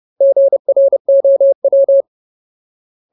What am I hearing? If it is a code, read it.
Morse code, heard as a keyed tone, about 30 wpm, 550 Hz. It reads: GROW
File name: Ultimate Word List / 30WPM Ultimate